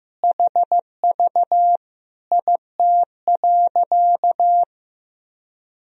Morse code, heard as a keyed tone, about 15 wpm, 700 Hz. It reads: HV IT.
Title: Sentences from Top 300 Words and Abbreviations 15wpm